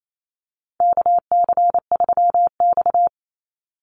Morse code, heard as a keyed tone, about 28 wpm, 690 Hz. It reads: KC3X
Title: Call Signs Repeated 3x - 28wpm